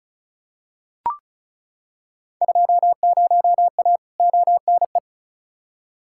Morse code, heard as a keyed tone, about 36 wpm, 700 Hz. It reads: 20A ONE
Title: Field Day Exchanges Encore - 35wpm